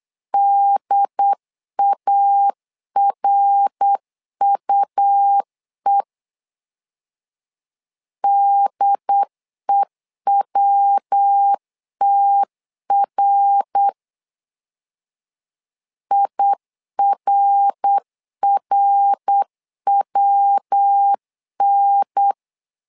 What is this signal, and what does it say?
Morse code, heard as a keyed tone, about 8 wpm, 800 Hz. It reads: DARUE DEWTR IRRWN
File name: LEZIONI / 3 MWRO / 3 MWR 8wpm 2444